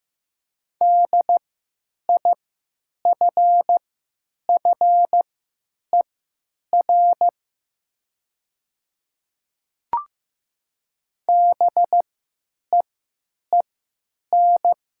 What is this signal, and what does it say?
Morse code, heard as a keyed tone, about 15 wpm, spaced as for 9 wpm, 700 Hz. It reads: DIFFER BEEN
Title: 3x Character Spacing - Top 200 Words - 15wpm